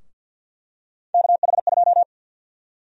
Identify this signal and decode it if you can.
Morse code, heard as a keyed tone, about 52 wpm, 700 Hz. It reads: KH2
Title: USA Call Sign Prefixes - Repeated 2x - 50wpm